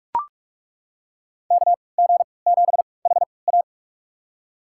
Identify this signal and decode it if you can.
Morse code, heard as a keyed tone, about 41 wpm, spaced as for 25 wpm, 700 Hz. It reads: KG7HA
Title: USA Call Signs - 5 Characters - 3x Character Spacing - 45wpm